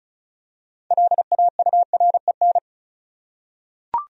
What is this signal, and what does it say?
Morse code, heard as a keyed tone, about 34 wpm, 700 Hz. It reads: LAUREN
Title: USA Common Names - 35wpm